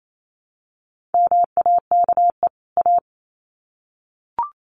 Morse code, heard as a keyed tone, about 28 wpm, 700 Hz. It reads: MAKE A
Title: Sets of 2 Words 28wpm